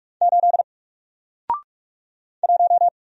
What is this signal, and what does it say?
Morse code, heard as a keyed tone, about 46 wpm, 700 Hz. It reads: 8 1
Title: Single Letter-Number Alt 45wpm